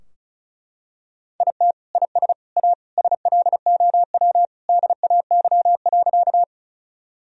Morse code, heard as a keyed tone, about 34 wpm, 700 Hz. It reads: IT IS A SLOW DAY.
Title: Sentences from Top 1000 Words - Review Part 2 - 35wpm